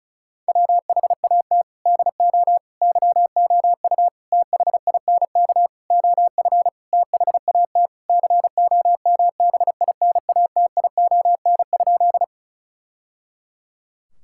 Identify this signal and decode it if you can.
Morse code, heard as a keyed tone, about 36 wpm, 700 Hz. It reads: WHAT DO YOU THINK OF THAT COMBINATION?
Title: Sentences from Top 1000 Words - Review Part 6 - 35wpm